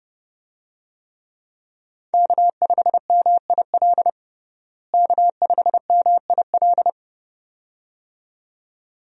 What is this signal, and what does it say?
Morse code, heard as a keyed tone, about 30 wpm, 700 Hz. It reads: K5MIL K5MIL
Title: USA Call Signs - 5 Characters - Repeated 2x - 30wpm